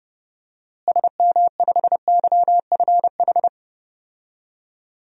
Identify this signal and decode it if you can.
Morse code, heard as a keyed tone, about 30 wpm, 700 Hz. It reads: SM5YFH
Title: DXCC-100 Call Signs - 6 Characters - Repeated 2x - 30wpm